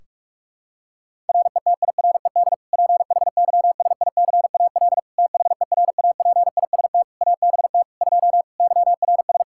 Morse code, heard as a keyed tone, about 46 wpm, 700 Hz. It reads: RETIRED PHYSICAL THERAPIST ABT 2 YRS
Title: K1USN SST QSO Practice 45wpm